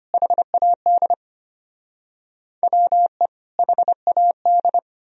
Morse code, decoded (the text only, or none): HAD WE HAD